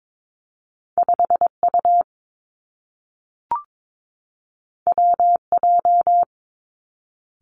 Morse code, heard as a keyed tone, about 22 wpm, 700 Hz. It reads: 5 U W J
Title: Two Challenging Letter-Number Combinations 22wpm